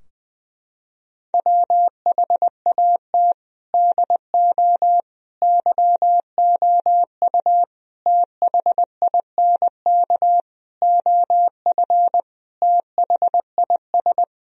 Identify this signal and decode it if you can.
Morse code, heard as a keyed tone, about 21 wpm, 700 Hz. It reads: WHAT DO YOU THINK OF THIS
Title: Sentences from Top 1000 Words - Review Part 2 - 20wpm